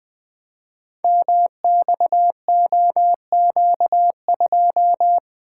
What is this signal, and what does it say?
Morse code, heard as a keyed tone, about 20 wpm, 700 Hz. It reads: MXOQ2
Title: Code Groups - 5 Letter:Numbers - 20wpm